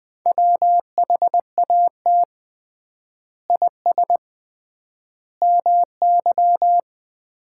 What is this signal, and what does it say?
Morse code, heard as a keyed tone, about 20 wpm, 700 Hz. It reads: WHAT IS MY
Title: Common QSO Phrases - 3x Word-Spacing - 20wpm